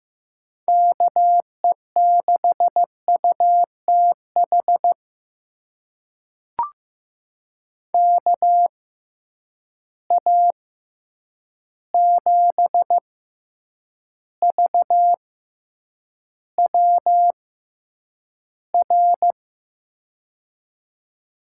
Morse code, heard as a keyed tone, about 15 wpm, spaced as for 8 wpm, 700 Hz. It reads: KE6UTH KA7VWR
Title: USA Call Signs - 6 Characters - 6x Character Spacing - 15wpm